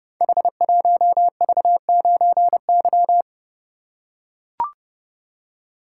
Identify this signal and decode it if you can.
Morse code, heard as a keyed tone, about 30 wpm, 700 Hz. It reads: H1V9Y
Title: Code Groups - 5 Letter:Numbers - 30wpm